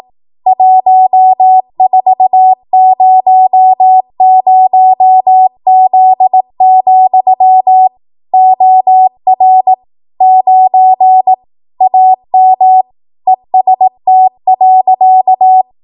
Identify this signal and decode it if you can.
Morse code, heard as a keyed tone, about 18 wpm, 750 Hz. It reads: 1400Z, OR 9 AM EST.
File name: Archive / 18 WPM / 18WPM